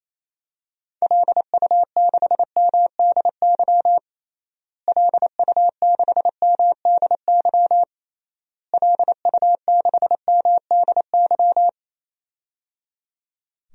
Morse code, heard as a keyed tone, about 28 wpm, 700 Hz. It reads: LU6MDY LU6MDY LU6MDY
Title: DXCC-100 Call Signs - 6 Characters - Repeated 3x - 28wpm